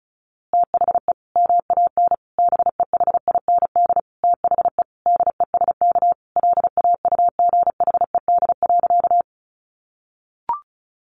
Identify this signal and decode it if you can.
Morse code, heard as a keyed tone, about 34 wpm, 700 Hz. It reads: THE MAN BEHIND THE DESK LAUGHED.
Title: A Sound Of Thunder 35wpm